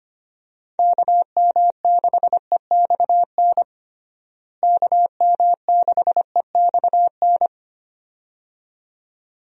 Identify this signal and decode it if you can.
Morse code, heard as a keyed tone, about 25 wpm, 700 Hz. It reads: KM6EXN KM6EXN
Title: USA Call Signs - 6 Characters - Repeated 2x - 25wpm